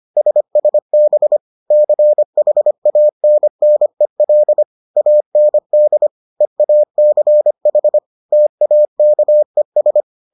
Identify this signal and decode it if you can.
Morse code, heard as a keyed tone, about 25 wpm, 600 Hz. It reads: SSB CHANNEL AND EACH TAKES